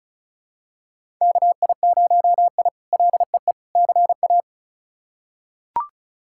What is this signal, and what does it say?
Morse code, heard as a keyed tone, about 34 wpm, 700 Hz. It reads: KI0I LEE CA
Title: K1USN SST QSO Practice - 35wpm